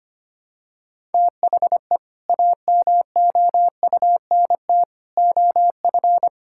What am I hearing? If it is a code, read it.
Morse code, heard as a keyed tone, about 25 wpm, 700 Hz. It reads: THE AMOUNT OF
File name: practice / 25wpm / Sentences with 3-Word Phrases - 25wpm